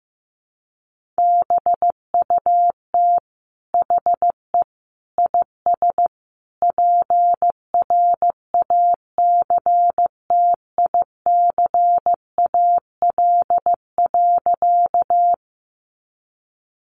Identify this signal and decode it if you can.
Morse code, heard as a keyed tone, about 15 wpm, 700 Hz. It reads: BUT HE IS PRACTICAL.
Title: Sentences from Top 1400 Words 15wpm